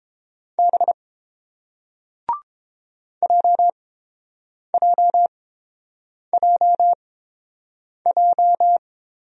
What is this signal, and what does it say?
Morse code, heard as a keyed tone, about 29 wpm, spaced as for 7 wpm, 700 Hz. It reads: B JJJJ